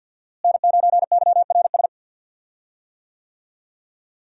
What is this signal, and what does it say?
Morse code, heard as a keyed tone, about 52 wpm, 700 Hz. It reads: N9YRS
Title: USA Call Signs - 5 Characters - Repeated 3x - 50wpm